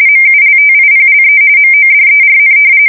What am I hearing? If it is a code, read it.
RTTY, about 45.45 baud. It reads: MULTIPLE MODERN
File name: fsk170 / usnavy1